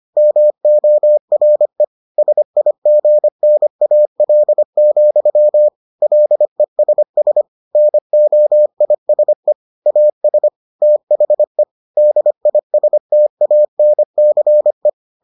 Morse code, heard as a keyed tone, about 25 wpm, 600 Hz. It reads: MORE SIGNAL, LESS NOISE AS THE DISTANCE